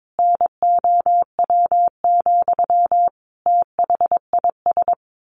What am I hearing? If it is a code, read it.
Morse code, heard as a keyed tone, about 22 wpm, 700 Hz. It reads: NOW, THIS
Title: A Sound Of Thunder 22wpm